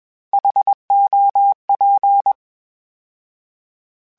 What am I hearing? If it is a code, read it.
Morse code, heard as a keyed tone, about 21 wpm, 800 Hz. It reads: HOP
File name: H Prefixes / H mp3s